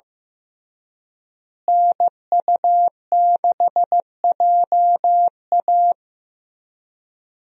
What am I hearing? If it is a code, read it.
Morse code, heard as a keyed tone, about 16 wpm, 700 Hz. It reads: NU6JA
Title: USA Call Signs - 5 Characters - Repeated 2x - 15wpm